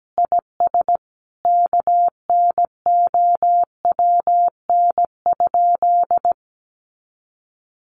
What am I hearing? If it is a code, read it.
Morse code, heard as a keyed tone, about 17 wpm, 700 Hz. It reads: IS KNOWN?
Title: Sentences from Top 600 Words 17wpm